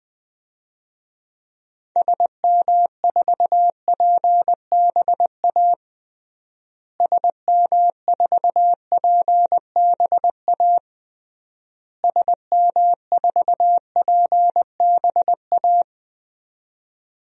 Morse code, heard as a keyed tone, about 21 wpm, 700 Hz. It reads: SM4PBA SM4PBA SM4PBA
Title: DXCC-100 Call Signs - 6 Characters - Repeated 3x - 20wpm